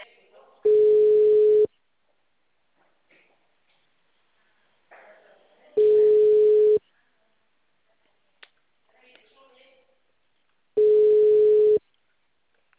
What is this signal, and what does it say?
Ringback tone (Europe, 425 Hz), with speech.